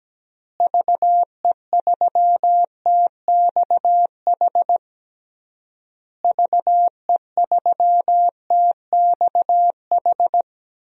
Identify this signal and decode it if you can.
Morse code, heard as a keyed tone, about 17 wpm, 700 Hz. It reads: VE3TXH VE3TXH
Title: DXCC-100 Call Signs - 6 Characters - Repeated 3x - 17wpm